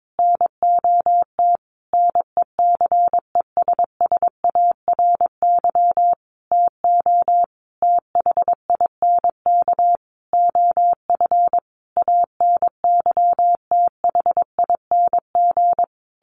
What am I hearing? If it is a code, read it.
Morse code, heard as a keyed tone, about 22 wpm, 700 Hz. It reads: NOT NECESSARY TO THINK OF ANYTHING
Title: Sentences from Top 1300 Words 22wpm